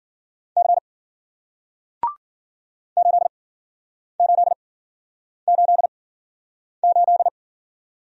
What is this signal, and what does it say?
Morse code, heard as a keyed tone, about 53 wpm, spaced as for 9 wpm, 700 Hz. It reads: C 8888